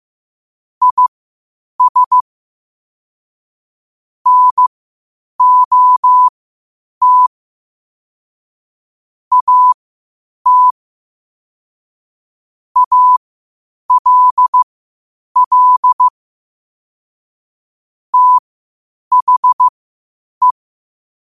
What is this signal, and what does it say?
Morse code, heard as a keyed tone, about 15 wpm, spaced as for 8 wpm, 1000 Hz. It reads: IS NOT AT ALL THE
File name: qotdfiles / 8wpm